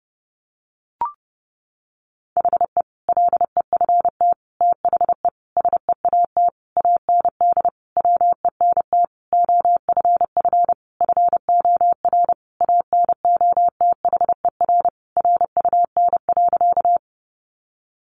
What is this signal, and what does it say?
Morse code, heard as a keyed tone, about 30 wpm, 700 Hz. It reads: HE LEFT THE SEAT AND WENT OFF FOR ANOTHER RUN.
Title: Sentences from Top 1500 Words 30wpm